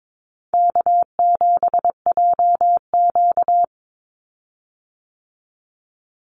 Morse code, heard as a keyed tone, about 22 wpm, 700 Hz. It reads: K7JQ
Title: Call Signs Repeated 3x - 22wpm